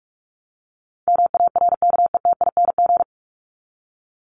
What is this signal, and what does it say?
Morse code, heard as a keyed tone, about 44 wpm, 700 Hz. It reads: MARKETING
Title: Top 751-1000 Words 45wpm